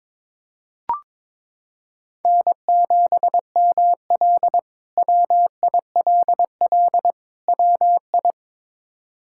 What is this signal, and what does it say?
Morse code, heard as a keyed tone, about 22 wpm, 700 Hz. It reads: N7ML WILL WI
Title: K1USN SST QSO Practice - 22wpm